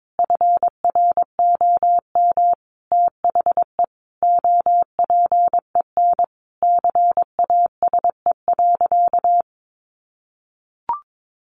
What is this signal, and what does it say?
Morse code, heard as a keyed tone, about 22 wpm, 700 Hz. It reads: FROM THE OPEN CASE.
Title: Sentences from Top 1000 Words 22wpm